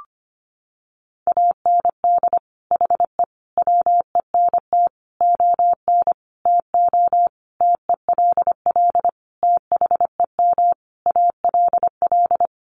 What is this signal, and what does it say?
Morse code, heard as a keyed tone, about 25 wpm, 700 Hz. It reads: AND HE WENT ON TO TELL THEM ALL
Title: Sentences from Top 900 Words 25wpm